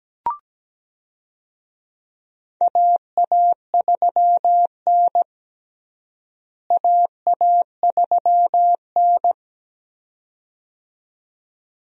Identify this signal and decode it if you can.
Morse code, heard as a keyed tone, about 17 wpm, 700 Hz. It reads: AA3N AA3N
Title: USA Call Signs - 4 Characters - Repeated 2x - 17wpm